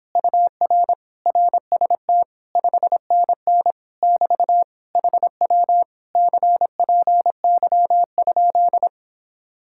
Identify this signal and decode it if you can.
Morse code, heard as a keyed tone, about 26 wpm, 700 Hz. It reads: UR RST 5NN <BT> HW CPY?